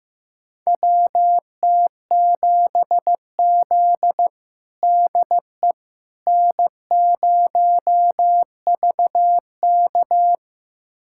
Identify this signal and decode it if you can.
Morse code, heard as a keyed tone, about 15 wpm, 700 Hz. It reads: WT7Z DE N0VK
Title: USA Callsign DE Callsign - 4 Characters - 15wpm